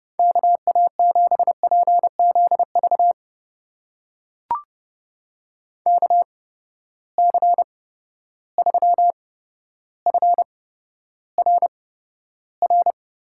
Morse code, heard as a keyed tone, about 30 wpm, spaced as for 13 wpm, 700 Hz. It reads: KA7PZV KC3FRR
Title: USA Call Signs - 6 Characters - 8x Character Spacing - 30wpm